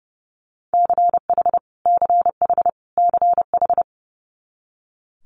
Morse code, heard as a keyed tone, about 30 wpm, 700 Hz. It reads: CH CH CH
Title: Multi-letter Phonemes Repeated 3x 30wpm